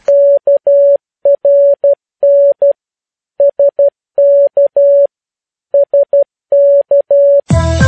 Morse code, heard as a keyed tone, about 12 wpm, 560 Hz. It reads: KRN SK SK
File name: PI4AA Uitzending